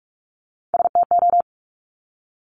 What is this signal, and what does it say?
Morse code, heard as a keyed tone, about 46 wpm, 700 Hz. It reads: STO
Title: Top Three Letter Combinations in Words 45wpm